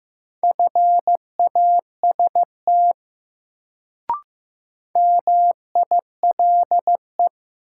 Morse code, heard as a keyed tone, about 15 wpm, 700 Hz. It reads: FAST MILE
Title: Sentences from Top 400 Words - New Word Drill - 15wpm